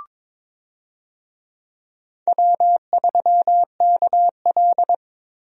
Morse code, heard as a keyed tone, about 22 wpm, 700 Hz. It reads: W3KL
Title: USA Call Signs - 4 Characters - Repeated 3x - 22wpm